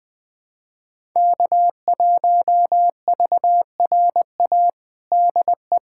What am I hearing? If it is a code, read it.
Morse code, heard as a keyed tone, about 21 wpm, 700 Hz. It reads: K1VRA DE
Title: USA Callsign DE Callsign - 5 Characters - 20wpm